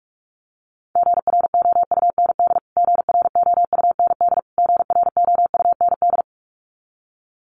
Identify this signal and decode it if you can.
Morse code, heard as a keyed tone, about 44 wpm, 700 Hz. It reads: GROUND GROUND GROUND
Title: Top 500 Words Repeated 45wpm